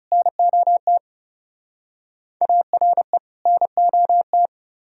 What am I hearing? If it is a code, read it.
Morse code, heard as a keyed tone, about 41 wpm, 700 Hz. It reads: NOT ARE NOT